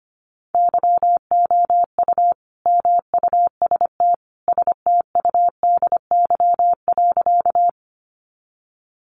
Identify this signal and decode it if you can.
Morse code, heard as a keyed tone, about 25 wpm, 700 Hz. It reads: YOU MUST STUDY.
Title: Sentences from Top 300 Words 25wpm